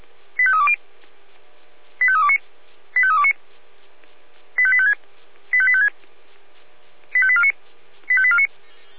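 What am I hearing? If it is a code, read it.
Five-tone selective call (ZVEI), heard as quick runs of five tones, tones of about 70 ms. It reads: 85320 85329 85329 85856 85856 85849 85849
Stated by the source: ZVEI five-tone call received by SDR, archived by feuerwehr-orlamuende.org.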